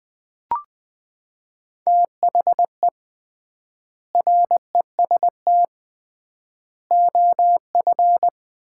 Morse code, heard as a keyed tone, about 21 wpm, 700 Hz. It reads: THE REST OF
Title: 3x Word-Spacing - Sets of 3 Words - 20wpm